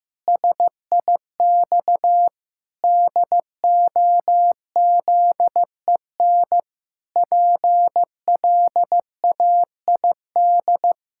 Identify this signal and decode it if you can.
Morse code, heard as a keyed tone, about 16 wpm, 700 Hz. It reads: SIX DOZEN PLAID
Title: Pangram Challenge v2 - 15wpm